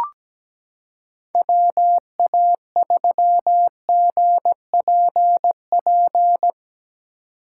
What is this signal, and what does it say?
Morse code, heard as a keyed tone, about 17 wpm, 700 Hz. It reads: WA3GPP
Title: K1USN SST QSO Practice - 17wpm